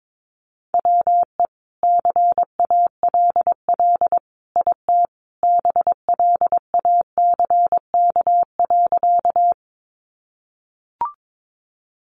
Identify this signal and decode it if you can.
Morse code, heard as a keyed tone, about 22 wpm, 700 Hz. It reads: WE CALL IT BLACK.
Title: Sentences from Top 400 Words 22wpm